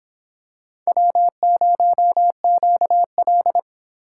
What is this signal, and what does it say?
Morse code, heard as a keyed tone, about 23 wpm, 700 Hz. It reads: W0QL